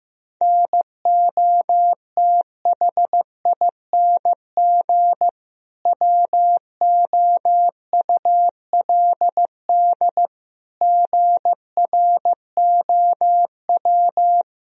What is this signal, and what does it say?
Morse code, heard as a keyed tone, about 15 wpm, 700 Hz. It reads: NOTHING WOULD GROW
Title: Sentences from Top 1000 Words - Review Part 3 - 15wpm